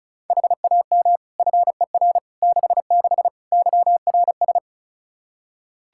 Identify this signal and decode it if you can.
Morse code, heard as a keyed tone, about 34 wpm, 700 Hz. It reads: HAM FER 66 YRS
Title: Practice QSO - Exchange 3 - More Info - 35wpm